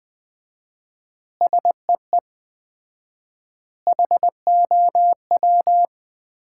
Morse code, heard as a keyed tone, about 20 wpm, 700 Hz. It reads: SEE HOW
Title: 4x Word-Spacing - Sets of 5 Words - 20wpm